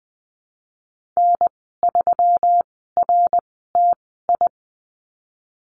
Morse code, heard as a keyed tone, about 20 wpm, spaced as for 14 wpm, 700 Hz. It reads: N3RTI
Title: 122 Lesson 20g - Number 3 - Call Signs 20wpm